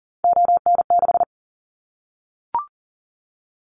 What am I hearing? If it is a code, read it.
Morse code, heard as a keyed tone, about 40 wpm, 700 Hz. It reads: ON6
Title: Callsign Prefixes for Top 25 DXCC - 40wpm